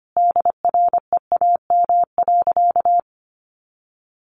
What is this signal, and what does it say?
Morse code, heard as a keyed tone, about 25 wpm, 700 Hz. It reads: DREAM.
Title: Sentences from Top 1500 Words 25wpm